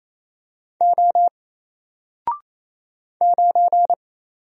Morse code, heard as a keyed tone, about 28 wpm, 700 Hz. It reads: O 9
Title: Single Letter-Number Alt 28wpm